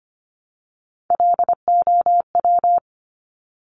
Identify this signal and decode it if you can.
Morse code, heard as a keyed tone, about 25 wpm, 700 Hz. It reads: LOW
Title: Top Three Letter Combinations in Words 25wpm